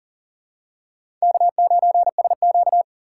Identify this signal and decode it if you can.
Morse code, heard as a keyed tone, about 40 wpm, 700 Hz. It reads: K9SQ
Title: USA Call Signs - 4 Characters - 2x Character Spacing - 40wpm